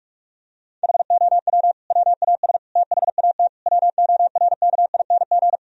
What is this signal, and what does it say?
Morse code, heard as a keyed tone, about 41 wpm, 700 Hz. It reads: HOW WAS THAT WORKING
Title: Sentences from Top 1000 Words - Review Part 4 - 45wpm